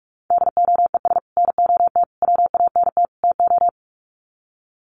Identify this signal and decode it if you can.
Morse code, heard as a keyed tone, about 44 wpm, 700 Hz. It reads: DOES NOT WANT TO
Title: Sets of 4 Words 45wpm